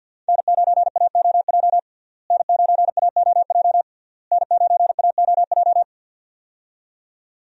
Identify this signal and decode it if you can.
Morse code, heard as a keyed tone, about 52 wpm, 700 Hz. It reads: N9AOJ N9AOJ N9AOJ
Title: USA Call Signs - 5 Characters - Repeated 3x - 50wpm